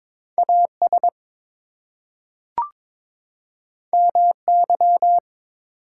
Morse code, heard as a keyed tone, about 22 wpm, 700 Hz. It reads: AS MY
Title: Two Letter Words - 22wpm